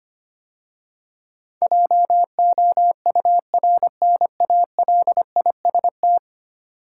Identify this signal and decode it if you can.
Morse code, heard as a keyed tone, about 25 wpm, 700 Hz. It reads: JOURNALIST
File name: 10 Letter Words - First 3 Letters Given - 25wpm